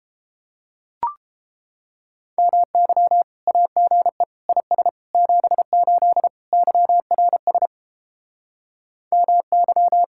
Morse code, heard as a keyed tone, about 33 wpm, 700 Hz. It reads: MY AGE IS 78 YRS MY